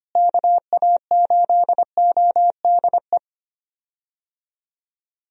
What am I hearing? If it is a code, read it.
Morse code, heard as a keyed tone, about 25 wpm, 700 Hz. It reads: KA8ODE
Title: USA Call Signs - 6 Characters - Repeated 2x - 25wpm